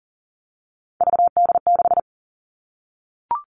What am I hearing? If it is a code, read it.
Morse code, heard as a keyed tone, about 40 wpm, 700 Hz. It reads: VD6
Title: All Cut Numbers - 3-Digit 40wpm